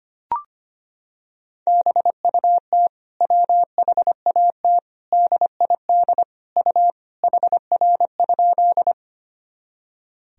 Morse code, heard as a keyed tone, about 25 wpm, 700 Hz. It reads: BUT WHAT DID U HR?
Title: Sentences from Top 300 Words and Abbreviations 25wpm